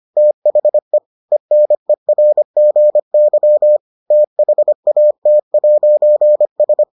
Morse code, heard as a keyed tone, about 25 wpm, 600 Hz. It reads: THE ENERGY THAT'S